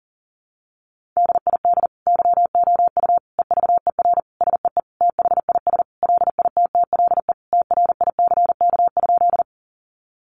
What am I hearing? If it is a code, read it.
Morse code, heard as a keyed tone, about 40 wpm, 700 Hz. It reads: DID YOU EVER SEE THIS LITTLE TRICK?
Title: Sentences from Top 1500 Words 40wpm